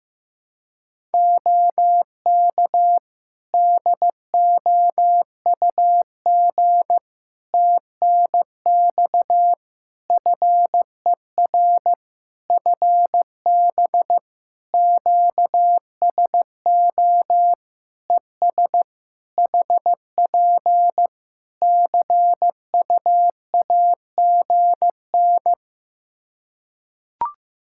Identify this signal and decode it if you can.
Morse code, heard as a keyed tone, about 15 wpm, 700 Hz. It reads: OK DOUG TNX FER FB QSO ES HP CUAGN
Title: Practice QSO - Exchange 4 - Ending - 15wpm